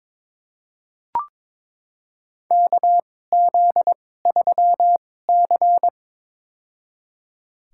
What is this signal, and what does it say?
Morse code, heard as a keyed tone, about 22 wpm, spaced as for 16 wpm, 700 Hz. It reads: KZ3C
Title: USA Call Signs - 2x Character Spacing - 22wpm